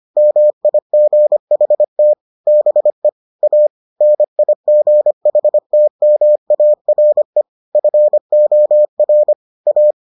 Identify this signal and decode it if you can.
Morse code, heard as a keyed tone, about 25 wpm, 600 Hz. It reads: MIGHT BE A NIGHTMARE FOR A